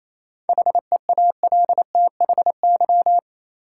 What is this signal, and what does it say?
Morse code, heard as a keyed tone, about 28 wpm, 700 Hz. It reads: HEALTHY
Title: Sentences from Top 1000 Words - Review Part 5 - New Word Drill - 28wpm